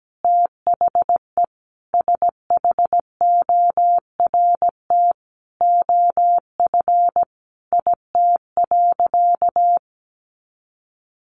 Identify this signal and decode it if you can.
Morse code, heard as a keyed tone, about 17 wpm, 700 Hz. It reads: THE SHORT OF IT.
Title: Sentences from Top 400 Words 17wpm